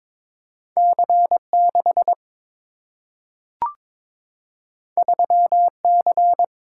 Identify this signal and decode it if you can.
Morse code, heard as a keyed tone, about 22 wpm, 700 Hz. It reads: C 6 3 C
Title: Single-Letter Single-Number 22wpm